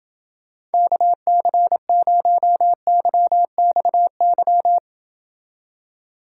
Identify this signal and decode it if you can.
Morse code, heard as a keyed tone, about 29 wpm, 700 Hz. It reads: KC0YXY